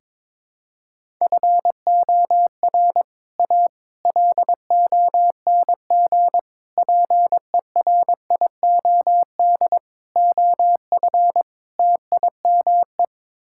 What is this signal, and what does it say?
Morse code, heard as a keyed tone, about 22 wpm, 700 Hz. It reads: FOR A LONG PERIOD OF TIME
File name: practice / 22wpm / Sets of 6 Words - 22wpm